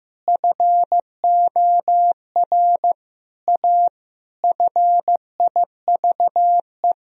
Morse code, heard as a keyed tone, about 15 wpm, 700 Hz. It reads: FOR A FIVE
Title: Sentences from Top 1000 Words - Review Part 2 - 15wpm